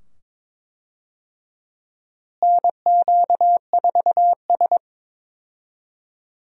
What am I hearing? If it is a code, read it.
Morse code, heard as a keyed tone, about 22 wpm, 700 Hz. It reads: NQ4S
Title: USA Call Signs - 4 Characters - Repeated 2x - 22wpm